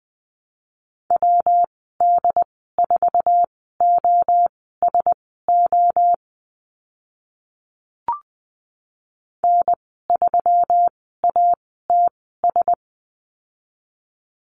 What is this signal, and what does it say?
Morse code, heard as a keyed tone, about 20 wpm, spaced as for 15 wpm, 700 Hz. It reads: WD4OSO N3ATS
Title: 122 Lesson 20g - Number 3 - Call Signs 20wpm